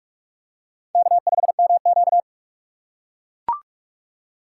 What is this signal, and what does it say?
Morse code, heard as a keyed tone, about 44 wpm, 700 Hz. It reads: K5MQ
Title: USA Call Signs - 4 Characters - 6x Character Spacing - 45wpm